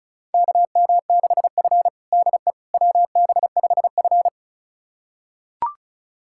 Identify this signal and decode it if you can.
Morse code, heard as a keyed tone, about 34 wpm, 700 Hz. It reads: KM6F DE WB5F
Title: USA Callsign DE Callsign - 4 Characters - 35wpm